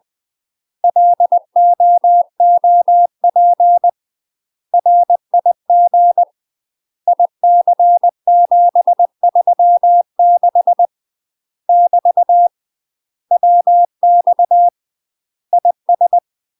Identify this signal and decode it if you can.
Morse code, heard as a keyed tone, about 20 wpm, 700 Hz. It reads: LOOP RIG IC736 <BT> WX IS
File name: UK MASTER QSOs TRAINING/20WPM